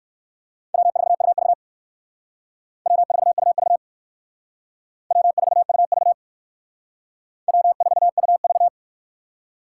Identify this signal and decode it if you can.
Morse code, heard as a keyed tone, about 57 wpm, 700 Hz. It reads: W4UV W4UV W4UV W4UV